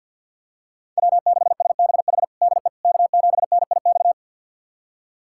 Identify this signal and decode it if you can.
Morse code, heard as a keyed tone, about 52 wpm, 700 Hz. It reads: W6SBH DE K7NIX
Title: USA Callsign DE Callsign 50wpm